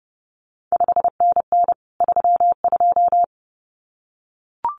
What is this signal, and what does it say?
Morse code, heard as a keyed tone, about 30 wpm, 700 Hz. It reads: 5NN 32
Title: CQ WW DX Contest Exchange Encore - 30wpm